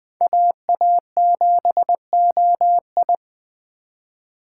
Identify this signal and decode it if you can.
Morse code, heard as a keyed tone, about 21 wpm, 700 Hz. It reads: AA7OI